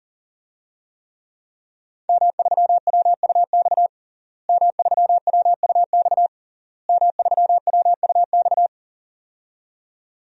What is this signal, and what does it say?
Morse code, heard as a keyed tone, about 40 wpm, 700 Hz. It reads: M3WUX M3WUX M3WUX
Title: DXCC-100 Call Signs - 5 Characters - Repeated 3x - 40wpm